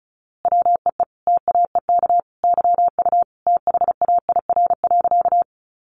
Morse code, heard as a keyed tone, about 34 wpm, 700 Hz. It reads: WEE TAEK YU THAIR.
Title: A Sound Of Thunder 35wpm